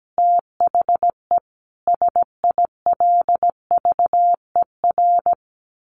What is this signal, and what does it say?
Morse code, heard as a keyed tone, about 17 wpm, 700 Hz. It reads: THE SILVER
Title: A Sound Of Thunder 17wpm